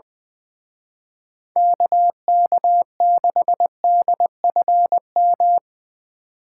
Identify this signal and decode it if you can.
Morse code, heard as a keyed tone, about 22 wpm, 700 Hz. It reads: KK6DFM
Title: Speed Racing - USA 2x3 Call Signs 20wpm